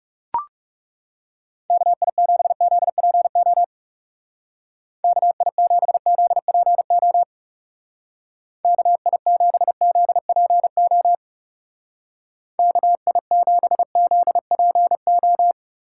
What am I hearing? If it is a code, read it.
Morse code, heard as a keyed tone, about 36 wpm, 700 Hz. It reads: KI7ZPO KI7ZPO KI7ZPO KI7ZPO
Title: Speed Racing - USA Call Signs 30wpm